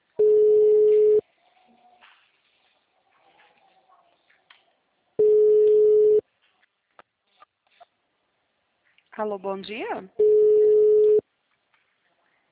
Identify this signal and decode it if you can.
Ringback tone (Europe, 425 Hz), with speech.
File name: out- -2012- - - .2